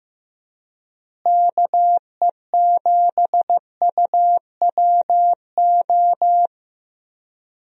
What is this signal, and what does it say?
Morse code, heard as a keyed tone, about 16 wpm, 700 Hz. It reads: KE7UWO
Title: USA Call Signs - 6 Characters - Repeated 3x - 15wpm